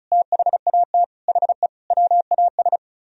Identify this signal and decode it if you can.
Morse code, heard as a keyed tone, about 47 wpm, 700 Hz. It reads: THAT HE WAS